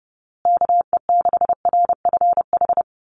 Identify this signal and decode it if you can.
Morse code, heard as a keyed tone, about 30 wpm, 700 Hz. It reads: KE6RFH
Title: USA Call Signs Repeated 3x 30wpm